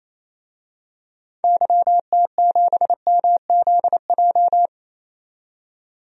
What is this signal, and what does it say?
Morse code, heard as a keyed tone, about 28 wpm, 700 Hz. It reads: YT7MZJ
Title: DXCC-100 Call Signs - 6 Characters - 28wpm